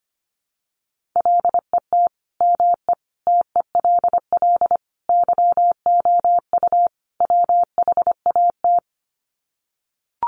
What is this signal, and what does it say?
Morse code, heard as a keyed tone, about 25 wpm, 700 Hz. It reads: LET ME TELL YOU WHAT
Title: Sets of 5 Words Encore 25wpm